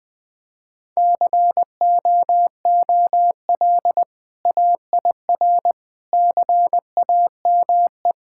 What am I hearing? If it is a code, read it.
Morse code, heard as a keyed tone, about 20 wpm, 700 Hz. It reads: COOL AIR CAME
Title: Sentences from Top 1600 Words 20wpm